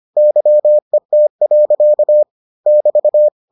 Morse code, heard as a keyed tone, about 25 wpm, 600 Hz. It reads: YET. <BT>